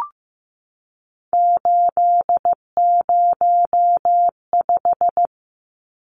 Morse code, heard as a keyed tone, about 15 wpm, 700 Hz. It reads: 805
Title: Numbers - 3-Digit 15wpm